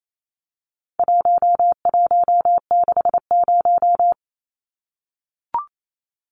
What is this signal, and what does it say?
Morse code, heard as a keyed tone, about 28 wpm, 700 Hz. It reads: 1160
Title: Numbers - 4-Digit 28wpm